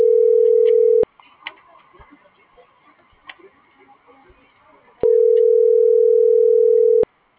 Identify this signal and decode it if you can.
Ringback tone (North America), 440 + 480 Hz.